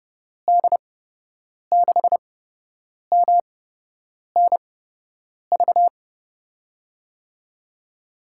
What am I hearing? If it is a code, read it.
Morse code, heard as a keyed tone, about 31 wpm, spaced as for 8 wpm, 700 Hz. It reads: D6MNV
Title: USA Call Signs - 6 Characters - 8x Character Spacing - 30wpm